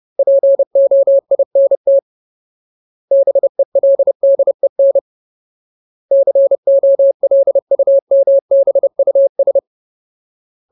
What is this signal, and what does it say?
Morse code, heard as a keyed tone, about 30 wpm, 550 Hz. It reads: POINT BELDEN COLUMBUS
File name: Ultimate Word List / 30WPM Ultimate